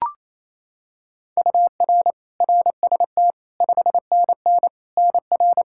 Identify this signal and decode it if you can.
Morse code, heard as a keyed tone, about 28 wpm, 700 Hz. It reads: UR RST 5NN NR
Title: Practice QSO - Exchange 1 - Beginning - 28wpm